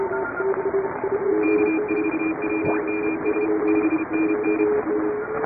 Morse code, heard as a keyed tone, about 32 wpm, 2400 Hz, with radio noise. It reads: K4WMS 5NN
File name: chatham / 2350-12CW